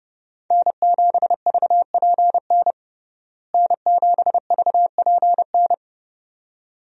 Morse code, heard as a keyed tone, about 31 wpm, 700 Hz. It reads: N7VPN N7VPN
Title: USA Call Signs - 5 Characters - Repeated 2x - 30wpm